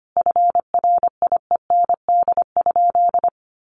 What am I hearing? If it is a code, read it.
Morse code, heard as a keyed tone, about 25 wpm, 700 Hz. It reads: FRIEND?
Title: Sentences from Top 300 Words 25wpm